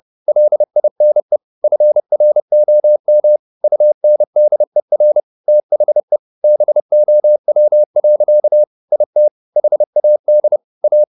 Morse code, heard as a keyed tone, about 30 wpm, 600 Hz. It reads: LINE FROM UNDER THE BOW. IT HAD A
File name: the old man and the sea / the old man and the sea 1 30wpm